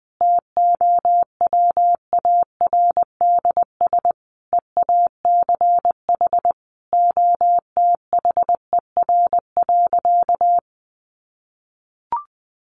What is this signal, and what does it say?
Morse code, heard as a keyed tone, about 20 wpm, 700 Hz. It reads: TOWARDS EACH OTHER.
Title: Sentences from Top 1200 Words 20wpm